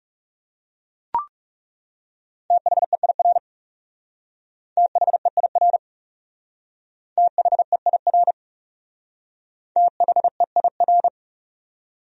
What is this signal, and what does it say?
Morse code, heard as a keyed tone, about 36 wpm, 700 Hz. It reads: THEIR THEIR THEIR THEIR
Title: Speed-Racing Top 200 Words 30wpm